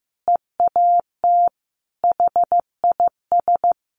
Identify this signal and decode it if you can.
Morse code, heard as a keyed tone, about 14 wpm, 700 Hz. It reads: EAT HIS
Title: Sentences from Top 1100 Words 15wpm